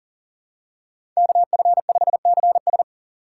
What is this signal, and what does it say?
Morse code, heard as a keyed tone, about 40 wpm, 700 Hz. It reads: KF5CS
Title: USA Call Signs - 8x Character Spacing - 40wpm